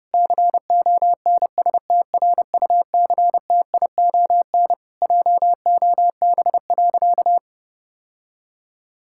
Morse code, heard as a keyed tone, about 30 wpm, 700 Hz. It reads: CONSTRUCTION JOB.
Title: Sentences from Top 1600 Words 30wpm